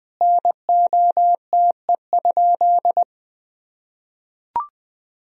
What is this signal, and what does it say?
Morse code, heard as a keyed tone, about 21 wpm, 700 Hz. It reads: NOTE?
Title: Sentences from Top 1000 Words - Review Part 2 - 20wpm